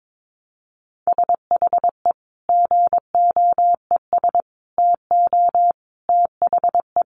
Morse code, heard as a keyed tone, about 22 wpm, 700 Hz. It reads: SHE GOES TO THE
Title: Easy Phrases 22wpm